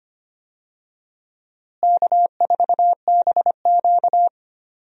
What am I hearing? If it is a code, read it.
Morse code, heard as a keyed tone, about 25 wpm, 700 Hz. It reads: K4BQ
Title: USA Call Signs - 4 Characters - Repeated 2x - 25wpm